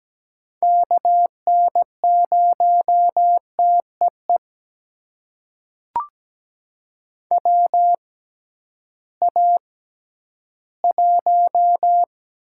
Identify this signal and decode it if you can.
Morse code, heard as a keyed tone, about 17 wpm, spaced as for 11 wpm, 700 Hz. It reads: KN0TEE WA1
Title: USA Call Signs - 6 Characters - 6x Character Spacing - 17wpm